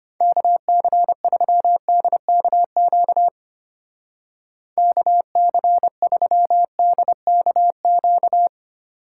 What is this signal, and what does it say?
Morse code, heard as a keyed tone, about 32 wpm, 700 Hz. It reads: KC3DKQ KC3DKQ